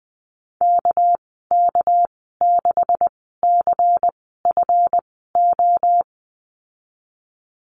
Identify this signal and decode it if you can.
Morse code, heard as a keyed tone, about 20 wpm, spaced as for 15 wpm, 700 Hz. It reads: KK6CFO
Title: 219 Lesson 35g - Number 0 - Call Signs 20wpm